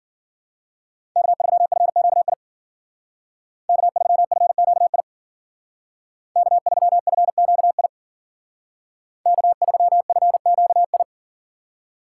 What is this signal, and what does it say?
Morse code, heard as a keyed tone, about 52 wpm, 700 Hz. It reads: K3FQI K3FQI K3FQI K3FQI